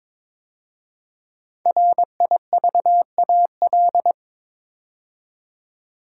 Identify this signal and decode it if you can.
Morse code, heard as a keyed tone, about 23 wpm, 700 Hz. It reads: RIVAL
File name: 5 Letter Words - First 3 Letters Given - 22wpm